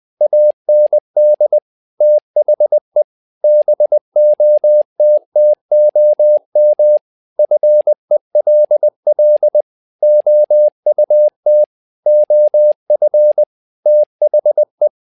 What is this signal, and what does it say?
Morse code, heard as a keyed tone, about 20 wpm, 600 Hz. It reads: AND THE BOTTOM FELL OUT OF THE
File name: learning center / tower trouble / tower trouble 20wpm